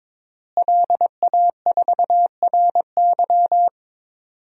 Morse code, heard as a keyed tone, about 22 wpm, 700 Hz. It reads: LA4RY
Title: DXCC-100 Call Signs - 5 Characters - Repeated 3x - 22wpm